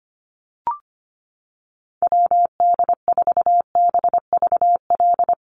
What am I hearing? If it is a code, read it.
Morse code, heard as a keyed tone, about 25 wpm, 700 Hz. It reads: WD4BVL
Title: USA Call Signs Repeated 2x 25wpm